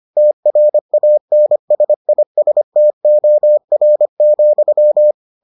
Morse code, heard as a keyed tone, about 25 wpm, 600 Hz. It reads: TRANSISTOR,